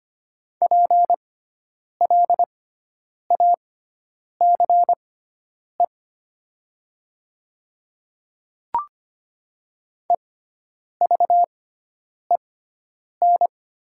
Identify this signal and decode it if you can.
Morse code, heard as a keyed tone, about 25 wpm, spaced as for 9 wpm, 700 Hz. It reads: PLACE EVEN
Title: 6x Character Spacing - Top 200 Words - 25wpm